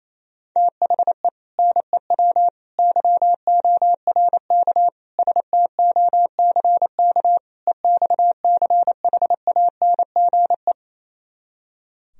Morse code, heard as a keyed tone, about 27 wpm, 700 Hz. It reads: THE NEW YORK STOCK EXCHANGE
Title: Speed-Racing - Sets of 5 Words 28wpm